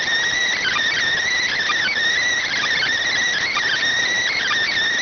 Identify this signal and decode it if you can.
Emergency Alert System (SAME) tones, heard as harsh buzzing bursts. It is heard from 0.4 to 4.5 s.